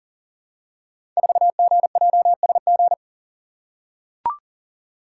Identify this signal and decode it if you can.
Morse code, heard as a keyed tone, about 39 wpm, 700 Hz. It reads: 4GJSG
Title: Code Groups - 5 Letter:Numbers - 40wpm